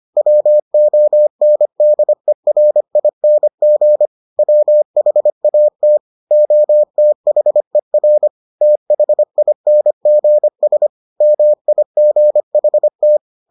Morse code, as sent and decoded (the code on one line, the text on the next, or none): .-- --- -. -.. .-. .. -. --. / .-- .... .- - / --- - .... .-. / - .... .. -. --. ... / -- .. --. .... -
WONDERING WHAT OTHER THINGS MIGHT